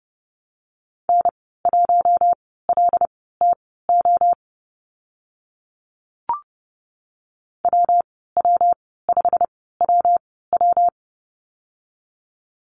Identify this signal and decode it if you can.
Morse code, heard as a keyed tone, about 30 wpm, spaced as for 17 wpm, 700 Hz. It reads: N1LTO WW5WW
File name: Lesson 19g - Letter w - Call Signs 30wpm